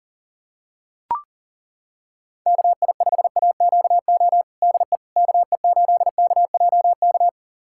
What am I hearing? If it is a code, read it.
Morse code, heard as a keyed tone, about 40 wpm, 700 Hz. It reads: KI5AQO DE KE8KJK
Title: USA Callsign DE Callsign 40wpm